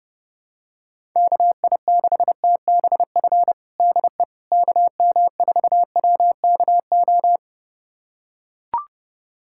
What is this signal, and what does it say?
Morse code, heard as a keyed tone, about 30 wpm, 700 Hz. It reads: KI6TBF DE KM4WKO
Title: USA Callsign DE Callsign 30wpm